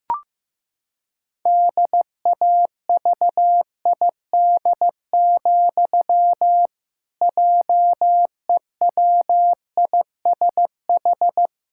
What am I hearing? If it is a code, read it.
Morse code, heard as a keyed tone, about 16 wpm, 700 Hz. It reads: DAVID, JEWISH
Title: Pangram Challenge v2 - 15wpm